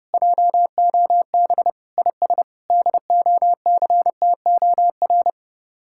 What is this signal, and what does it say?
Morse code, heard as a keyed tone, about 37 wpm, 700 Hz. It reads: JOB IS DOCTOR
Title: Speed-Racing Common QSO Phrases - 30wpm